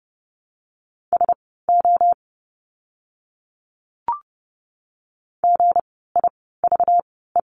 Morse code, heard as a keyed tone, about 30 wpm, spaced as for 17 wpm, 700 Hz. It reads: SO GIVE
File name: course / 30:10wpm / 230 Lesson 37f - Letter x - Words 30wpm